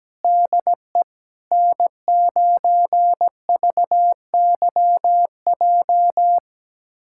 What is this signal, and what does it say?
Morse code, heard as a keyed tone, about 17 wpm, 700 Hz. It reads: DE N9VYJ
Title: USA Callsign DE Callsign - 5 Characters - 17wpm